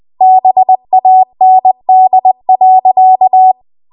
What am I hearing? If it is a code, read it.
Morse code, heard as a keyed tone, about 20 wpm, 750 Hz. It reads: BAND.
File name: Archive / 20 WPM / 20WPM